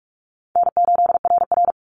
Morse code, heard as a keyed tone, about 44 wpm, 700 Hz. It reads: N8RR
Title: Easy Call Signs Repeated 2x - 45wpm